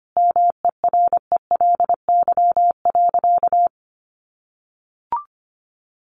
Morse code, heard as a keyed tone, about 25 wpm, 700 Hz. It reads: MERELY.
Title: Sentences from Top 1400 Words 25wpm